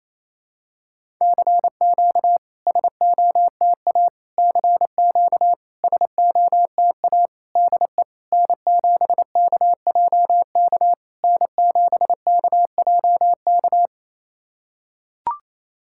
Morse code, heard as a keyed tone, about 28 wpm, 690 Hz. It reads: CQ SOTA CQ SOTA DE N7KJK N7KJK
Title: SOTA QSO Practice 28wpm